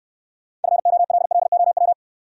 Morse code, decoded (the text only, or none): FOLLOW